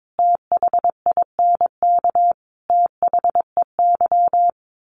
Morse code, heard as a keyed tone, about 22 wpm, 700 Hz. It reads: THINK THEY
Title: Sets of 3 Words 22wpm